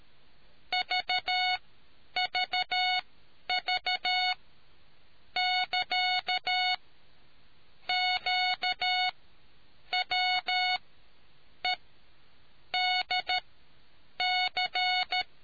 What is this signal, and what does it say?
Morse code, heard as a keyed tone, about 13 wpm, spaced as for 7 wpm, 730 Hz. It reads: VVV<KA>QWEDC